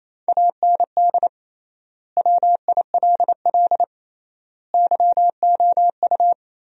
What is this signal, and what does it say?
Morse code, heard as a keyed tone, about 28 wpm, 700 Hz. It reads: AND WILL YOU
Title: 3x Word-Spacing - Sentences from Top 400 Words - 28wpm